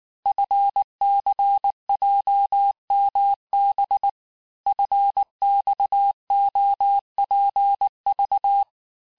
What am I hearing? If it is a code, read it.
Morse code, heard as a keyed tone, about 19 wpm, 800 Hz. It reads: FCJMB FXOPV
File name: cours-du-lundi files / cours-1595-teste-CHIFFRE-vitesse18-MPM